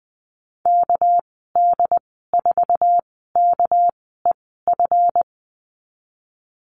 Morse code, heard as a keyed tone, about 19 wpm, spaced as for 14 wpm, 700 Hz. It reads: KD4KEF
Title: 219 Lesson 35g - Number 0 - Call Signs 20wpm